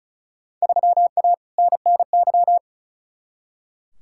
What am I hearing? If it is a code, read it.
Morse code, heard as a keyed tone, about 36 wpm, 700 Hz. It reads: 3A NNY
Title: Field Day Exchanges - 35wpm